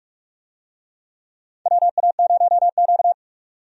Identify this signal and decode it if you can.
Morse code, heard as a keyed tone, about 46 wpm, 700 Hz. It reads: WA0Q
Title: USA Call Signs - 4 Characters - 45wpm